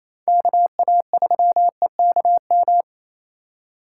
Morse code, heard as a keyed tone, about 28 wpm, 700 Hz. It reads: KA3EKM
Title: USA Call Signs - 4x Character Spacing - 28wpm